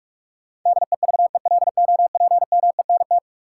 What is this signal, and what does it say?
Morse code, heard as a keyed tone, about 42 wpm, 700 Hz. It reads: DEVELOPMENT
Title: Sentences from Top 1400 Words - New Word Drill Alt - 45wpm